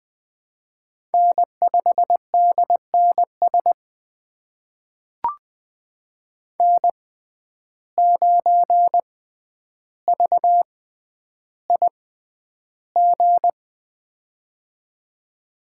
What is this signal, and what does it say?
Morse code, heard as a keyed tone, about 20 wpm, spaced as for 10 wpm, 700 Hz. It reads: N5DNS N9VIG
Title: USA Call Signs - 5 Characters - 6x Character Spacing - 20wpm